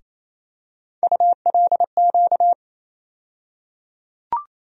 Morse code, heard as a keyed tone, about 28 wpm, 700 Hz. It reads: ULQ
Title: USA Call Sign Postfixes - 3 Letters 28wpm